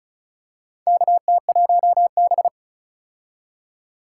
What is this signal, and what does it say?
Morse code, heard as a keyed tone, about 34 wpm, 700 Hz. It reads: KT1B
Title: K1USN SST QSO Practice - 35wpm